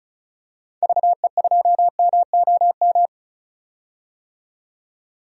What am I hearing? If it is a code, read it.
Morse code, heard as a keyed tone, about 36 wpm, 700 Hz. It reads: VE2MOM
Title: DXCC-100 Call Signs - 6 Characters - 35wpm